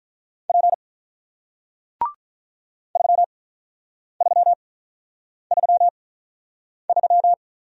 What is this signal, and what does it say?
Morse code, heard as a keyed tone, about 48 wpm, spaced as for 9 wpm, 700 Hz. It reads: P 3333